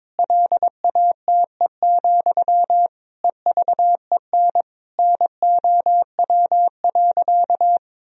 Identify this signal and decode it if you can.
Morse code, heard as a keyed tone, about 21 wpm, 700 Hz. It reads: LATE, EVEN NOW.
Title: Sentences from Top 1000 Words - Review Part 1 - 22wpm